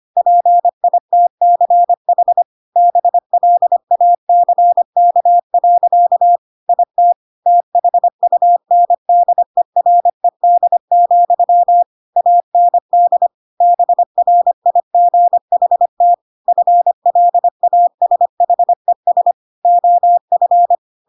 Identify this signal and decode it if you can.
Morse code, heard as a keyed tone, about 25 wpm, 700 Hz. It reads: PITCH BLACK. IT THUNDERED, AND BRIGHT FLASHES OF
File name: Pinocchio / Pinocchio ch6 25 25 ik1hge